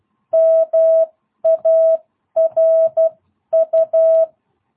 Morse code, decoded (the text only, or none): MARU